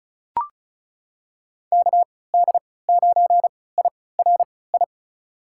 Morse code, heard as a keyed tone, about 36 wpm, spaced as for 20 wpm, 700 Hz. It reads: KD9IRI
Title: USA Call Signs - 6 Characters - 3x Character Spacing - 35wpm